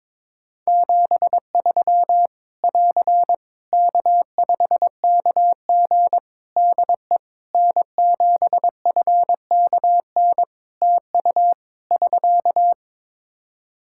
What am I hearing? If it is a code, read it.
Morse code, heard as a keyed tone, about 22 wpm, 700 Hz. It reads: 73 <AR> K5KG DE N7FKN TU <SK>
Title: K1USN SST QSO Practice 22wpm